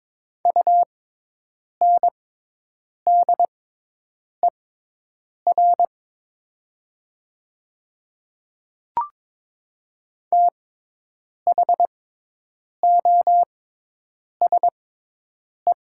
Morse code, heard as a keyed tone, about 22 wpm, spaced as for 8 wpm, 700 Hz. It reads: UNDER THOSE
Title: 6x Character Spacing - Top 300 Words - 22wpm